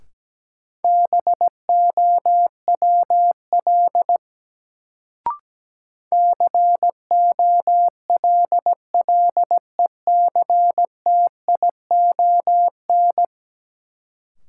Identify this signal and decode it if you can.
Morse code, heard as a keyed tone, about 17 wpm, 700 Hz. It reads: BOWL COLLECTION
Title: Sentences from Top 1200 Words - New Word Drill - 17wpm